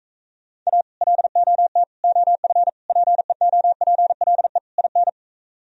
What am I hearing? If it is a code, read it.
Morse code, heard as a keyed tone, about 43 wpm, 700 Hz. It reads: A LOT OF PEOPLE IN